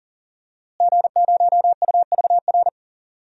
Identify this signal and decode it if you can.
Morse code, heard as a keyed tone, about 38 wpm, 700 Hz. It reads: G0UVR
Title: DXCC-100 Call Signs - 5 Characters - Repeated 3x - 40wpm